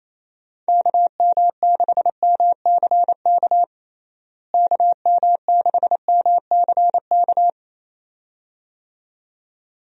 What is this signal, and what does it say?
Morse code, heard as a keyed tone, about 28 wpm, 700 Hz. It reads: KM6MCK KM6MCK
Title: USA Call Signs - 6 Characters - Repeated 2x - 28wpm